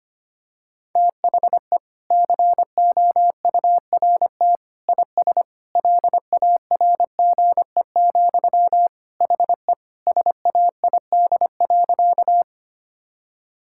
Morse code, heard as a keyed tone, about 25 wpm, 700 Hz. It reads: THE COURT IS LARGE, HE SAID.
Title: Sentences from Top 1600 Words 25wpm